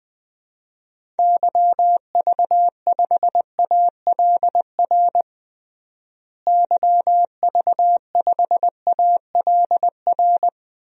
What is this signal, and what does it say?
Morse code, heard as a keyed tone, about 20 wpm, 700 Hz. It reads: YV5ALR YV5ALR
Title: DXCC-100 Call Signs - 6 Characters - Repeated 2x - 20wpm